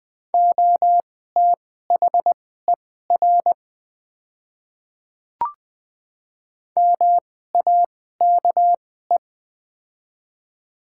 Morse code, heard as a keyed tone, about 20 wpm, spaced as for 14 wpm, 690 Hz. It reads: OTHER MAKE
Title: 2x Character Spacing - Top 100 Words - 20wpm